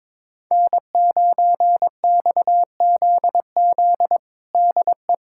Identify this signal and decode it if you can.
Morse code, heard as a keyed tone, about 22 wpm, 700 Hz. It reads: N9XZZ DE
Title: USA Callsign DE Callsign - 5 Characters - 22wpm